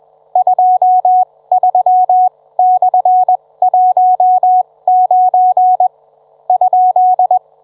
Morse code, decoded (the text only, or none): 23/19 ?